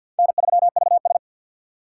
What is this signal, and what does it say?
Morse code, heard as a keyed tone, about 53 wpm, 690 Hz. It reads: N3VS
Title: USA Call Signs - 2x Character Spacing - 50wpm